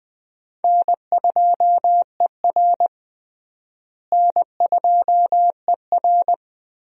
Morse code, heard as a keyed tone, about 20 wpm, 700 Hz. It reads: N2ER N2ER
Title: USA Call Signs - 4 Characters - Repeated 3x - 20wpm